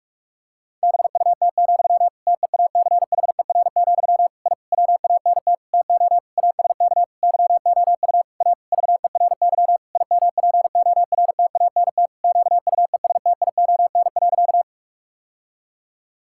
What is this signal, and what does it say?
Morse code, heard as a keyed tone, about 46 wpm, 700 Hz. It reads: BUT, TEACHER, I WANT TO ASK YOU A VERY IMPORTANT QUESTION.
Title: Sentences from Top 1000 Words - Review Part 6 - 45wpm